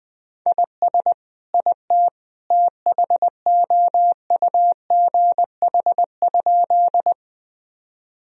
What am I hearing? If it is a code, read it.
Morse code, heard as a keyed tone, about 20 wpm, 700 Hz. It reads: IS IT THOUGH?
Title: Sentences from Top 1000 Words - Review Part 2 - 20wpm